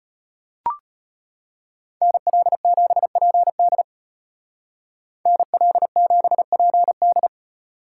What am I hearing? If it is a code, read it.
Morse code, heard as a keyed tone, about 31 wpm, 700 Hz. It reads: NL7PD NL7PD